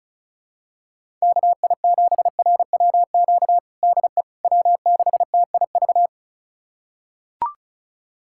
Morse code, heard as a keyed tone, about 36 wpm, 700 Hz. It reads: KI7RWQ DE W6TIV
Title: USA Callsign DE Callsign 35wpm